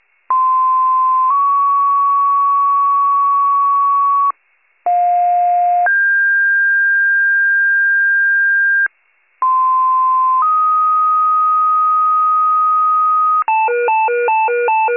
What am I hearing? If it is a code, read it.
Two-tone page, heard as a one-second tone, then a three-second tone, about 1040 then 1120 Hz, about 710 then 1600 Hz, about 1040 then 1250 Hz, with noise. It is heard from 0.3 to 13.4 s.